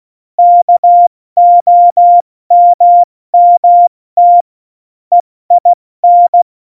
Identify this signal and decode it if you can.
Morse code, heard as a keyed tone, about 16 wpm, 700 Hz. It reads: KOMMT EIN
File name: Morsekurs / Witze / Witz 06 March 2018, 70 F80